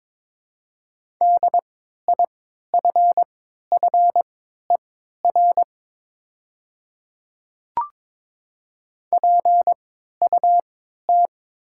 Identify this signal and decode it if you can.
Morse code, heard as a keyed tone, about 22 wpm, spaced as for 13 wpm, 700 Hz. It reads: DIFFER PUT
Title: 3x Character Spacing - Top 200 Words - 22wpm